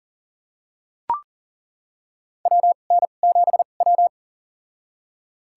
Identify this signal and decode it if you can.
Morse code, heard as a keyed tone, about 37 wpm, spaced as for 28 wpm, 700 Hz. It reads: WN7W
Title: USA Call Signs - 4 Characters - 2x Character Spacing - 40wpm